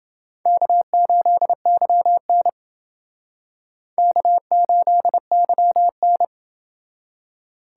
Morse code, heard as a keyed tone, about 29 wpm, 700 Hz. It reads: K8YN K8YN